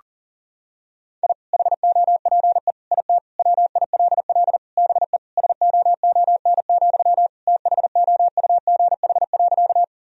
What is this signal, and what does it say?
Morse code, heard as a keyed tone, about 37 wpm, 700 Hz. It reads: I HOPE IT WILL BE SOON, THOUGH.
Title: Sentences from Top 1000 Words - Review Part 2 - 40wpm